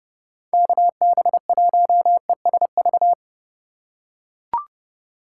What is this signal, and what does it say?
Morse code, heard as a keyed tone, about 29 wpm, 700 Hz. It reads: KB1ESV